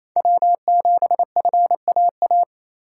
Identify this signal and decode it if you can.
Morse code, heard as a keyed tone, about 27 wpm, 700 Hz. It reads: W7FAA
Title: USA Call Signs - 4x Character Spacing - 28wpm